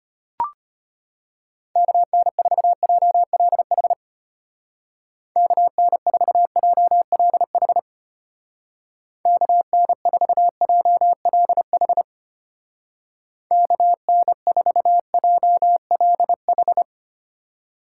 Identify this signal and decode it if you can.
Morse code, heard as a keyed tone, about 31 wpm, 700 Hz. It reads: KN4JLH KN4JLH KN4JLH KN4JLH
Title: Speed Racing - USA 2x3 Call Signs 25wpm